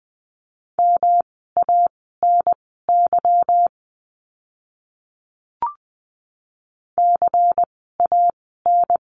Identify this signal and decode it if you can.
Morse code, heard as a keyed tone, about 20 wpm, spaced as for 15 wpm, 700 Hz. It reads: MANY CAN
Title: 146 Lesson 24f - Letter y - Words 20wpm